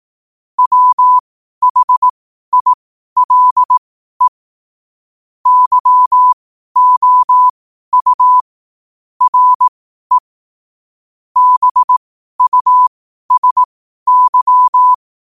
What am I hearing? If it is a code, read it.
Morse code, heard as a keyed tone, about 18 wpm, spaced as for 12 wpm, 1000 Hz. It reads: WHILE YOU RE BUSY